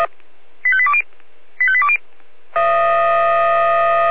Five-tone selective call (ZVEI), heard as quick runs of five tones, tones of about 70 ms. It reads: 85820 85820